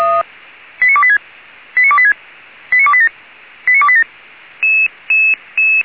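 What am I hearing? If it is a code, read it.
Five-tone selective call (ZVEI), heard as quick runs of five tones, tones of about 70 ms. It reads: 89286 89286 89287 89287
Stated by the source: ZVEI five-tone selective call received by SDR, channel 464.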